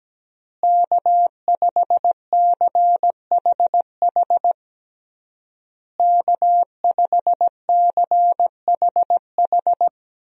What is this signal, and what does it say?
Morse code, heard as a keyed tone, about 17 wpm, 700 Hz. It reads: K5CHH K5CHH
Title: USA Call Signs - 5 Characters - Repeated 3x - 17wpm